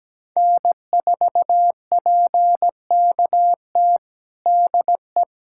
Morse code, heard as a keyed tone, about 17 wpm, 700 Hz. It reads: N4PKT DE
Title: USA Callsign DE Callsign - 5 Characters - 17wpm